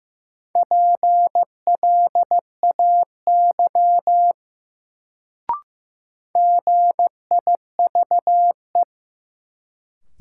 Morse code, heard as a keyed tone, about 15 wpm, 700 Hz. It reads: PLAY GIVE
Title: Sentences from Top 200 Words - New Word Drill - 15wpm